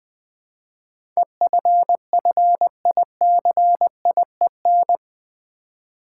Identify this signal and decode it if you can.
Morse code, heard as a keyed tone, about 20 wpm, 700 Hz. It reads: EFFICIEN
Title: Build A Word - 4 Letters At A Time - 10-Letter Words - 20wpm